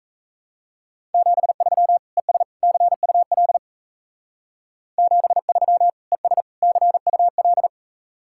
Morse code, heard as a keyed tone, about 36 wpm, 700 Hz. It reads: 73 ES CUL 73 ES CUL